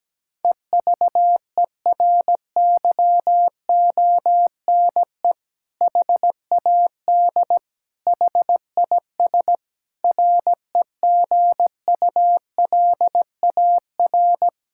Morse code, heard as a keyed tone, about 17 wpm, 700 Hz. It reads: EVERYONE HAD HIS REGULAR
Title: Sentences from Top 1000 Words - Review Part 6 - 17wpm